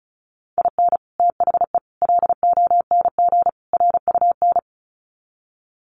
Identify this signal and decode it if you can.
Morse code, heard as a keyed tone, about 34 wpm, 700 Hz. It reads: IN THE LONG RUN
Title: Sets of 4 Words 35wpm